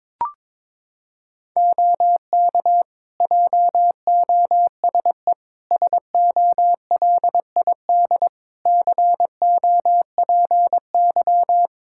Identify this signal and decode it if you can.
Morse code, heard as a keyed tone, about 22 wpm, 700 Hz. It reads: OK JOSE SOLID COPY
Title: Practice QSO - Exchange 3 - More Info - 22wpm